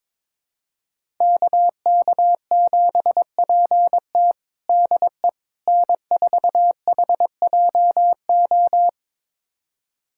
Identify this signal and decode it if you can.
Morse code, heard as a keyed tone, about 21 wpm, 700 Hz. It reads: KK7PT DE N4HJO
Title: USA Callsign DE Callsign - 5 Characters - 22wpm